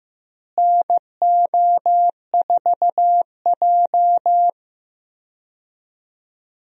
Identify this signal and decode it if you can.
Morse code, heard as a keyed tone, about 15 wpm, 700 Hz. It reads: NO4J
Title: USA Call Signs - 4 Characters - Repeated 3x - 15wpm